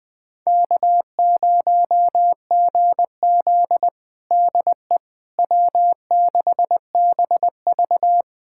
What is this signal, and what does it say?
Morse code, heard as a keyed tone, about 20 wpm, 700 Hz. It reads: K0GZ DE W6BV
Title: USA Callsign DE Callsign - 4 Characters - 20wpm